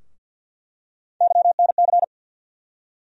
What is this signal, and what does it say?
Morse code, heard as a keyed tone, about 48 wpm, 700 Hz. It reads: YNC
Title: USA Call Sign Postfixes - 3 Letters 50wpm